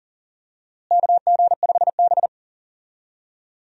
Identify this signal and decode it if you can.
Morse code, heard as a keyed tone, about 38 wpm, 700 Hz. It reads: KG5B
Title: USA Call Signs - 4 Characters - Repeated 2x - 40wpm